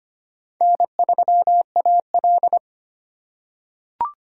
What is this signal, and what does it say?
Morse code, heard as a keyed tone, about 25 wpm, 700 Hz. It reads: N3AL
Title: USA Call Signs - 4x Character Spacing - 25wpm